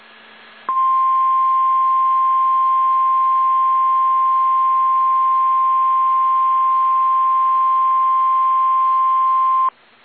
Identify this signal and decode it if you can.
Emergency Alert System (SAME) tones (the 1050 Hz attention tone). It is heard from 0.6 to 9.8 s.
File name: eas / KSDZKDJL / KSDZKDJL